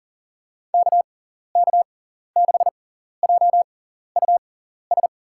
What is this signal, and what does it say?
Morse code, heard as a keyed tone, about 40 wpm, spaced as for 14 wpm, 700 Hz. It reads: KK6JUS
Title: USA Call Signs - 6x Character Spacing - 40wpm